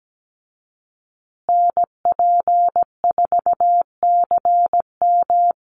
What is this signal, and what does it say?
Morse code, heard as a keyed tone, about 17 wpm, 700 Hz. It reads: NP4CM
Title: USA Call Signs Repeated 3x 17wpm